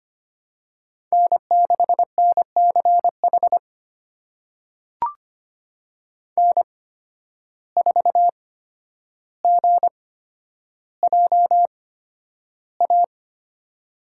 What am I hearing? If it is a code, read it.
Morse code, heard as a keyed tone, about 25 wpm, spaced as for 11 wpm, 700 Hz. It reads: N6NCH N4GJA
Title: USA Call Signs - 5 Characters - 8x Character Spacing - 25wpm